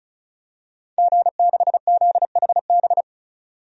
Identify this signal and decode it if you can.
Morse code, heard as a keyed tone, about 36 wpm, 700 Hz. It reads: G6ZHB
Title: DXCC-100 Call Signs - 5 Characters - 35wpm